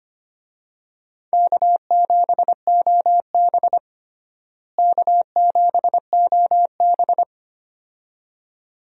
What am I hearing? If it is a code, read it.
Morse code, heard as a keyed tone, about 25 wpm, 700 Hz. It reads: K7OB K7OB
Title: USA Call Signs - 4 Characters - Repeated 2x - 25wpm